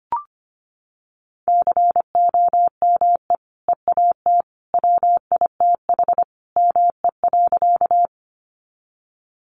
Morse code, heard as a keyed tone, about 25 wpm, 700 Hz. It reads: COME EAT WITH ME.
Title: Sentences from Top 300 Words 25wpm